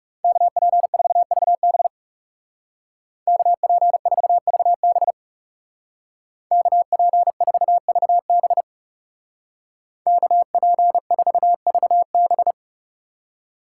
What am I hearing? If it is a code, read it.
Morse code, heard as a keyed tone, about 36 wpm, 700 Hz. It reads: KP4VB KP4VB KP4VB KP4VB
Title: Speed Racing - USA Call Signs 30wpm